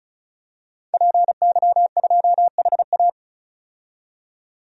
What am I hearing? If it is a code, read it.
Morse code, heard as a keyed tone, about 36 wpm, 700 Hz. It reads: PY2HA
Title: DXCC-100 Call Signs - 5 Characters - 35wpm